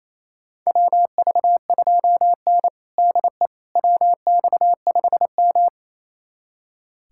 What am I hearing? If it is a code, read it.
Morse code, heard as a keyed tone, about 28 wpm, 700 Hz. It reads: WV2N DE WX5M
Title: USA Callsign DE Callsign - 4 Characters - 28wpm